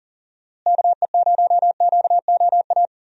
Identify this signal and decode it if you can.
Morse code, heard as a keyed tone, about 40 wpm, 700 Hz. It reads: KE0QOA
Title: USA Call Signs - 6 Characters - Repeated 3x - 40wpm